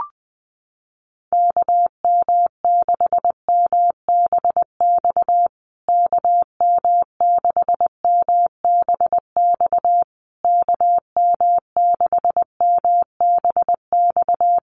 Morse code, heard as a keyed tone, about 20 wpm, 700 Hz. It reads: KM6MBX KM6MBX KM6MBX
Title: USA Call Signs Repeated 3x 20wpm